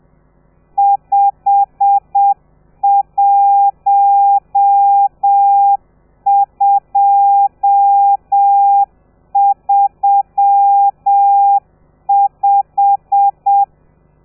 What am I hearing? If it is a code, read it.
Morse code, heard as a keyed tone, about 7 wpm, 800 Hz. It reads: 51235
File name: telegrafia / i2tfj / Lez12 30